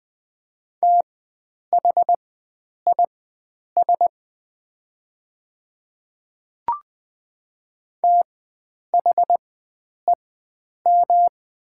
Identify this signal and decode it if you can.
Morse code, heard as a keyed tone, about 20 wpm, spaced as for 9 wpm, 700 Hz. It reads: THIS THEM
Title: 4x Character Spacing - Top 100 Words - 20wpm